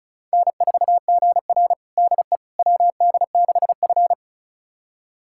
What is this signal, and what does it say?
Morse code, heard as a keyed tone, about 36 wpm, 700 Hz. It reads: N4GR DE WD6F
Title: USA Callsign DE Callsign - 4 Characters - 35wpm